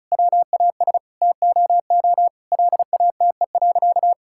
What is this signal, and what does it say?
Morse code, heard as a keyed tone, about 34 wpm, 700 Hz. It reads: WAS TOO LATE.
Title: Sentences from Top 1000 Words - Review Part 6 - 35wpm